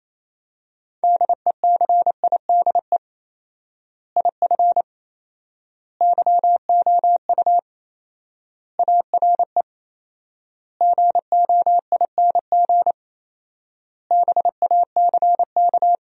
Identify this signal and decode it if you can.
Morse code, heard as a keyed tone, about 28 wpm, 700 Hz. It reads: DECIDE IF YOU ARE GOING BACK
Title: 4x Word-Spacing - Sentences from Top 500 Words - 28wpm